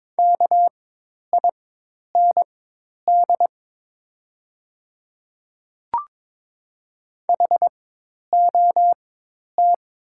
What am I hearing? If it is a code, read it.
Morse code, heard as a keyed tone, about 22 wpm, spaced as for 10 wpm, 690 Hz. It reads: KIND HOT
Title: 4x Character Spacing - Top 300 Words - 22wpm